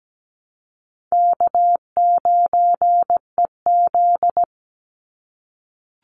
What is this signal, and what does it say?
Morse code, heard as a keyed tone, about 17 wpm, 700 Hz. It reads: K9EZ
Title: NA SOTA Chaser Callsigns 17wpm